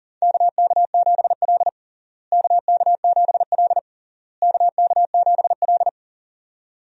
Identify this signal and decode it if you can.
Morse code, heard as a keyed tone, about 40 wpm, 700 Hz. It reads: KK7L KK7L KK7L
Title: USA Call Signs - 4 Characters - Repeated 3x - 40wpm